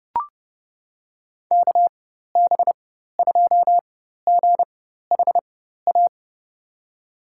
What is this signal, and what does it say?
Morse code, heard as a keyed tone, about 30 wpm, spaced as for 14 wpm, 700 Hz. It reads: KB2GHA
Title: USA Call Signs - 6 Characters - 4x Character Spacing - 30wpm